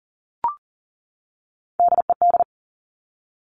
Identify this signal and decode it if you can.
Morse code, heard as a keyed tone, about 40 wpm, 700 Hz. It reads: DED
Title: Top Three Letter Combinations in Words 40wpm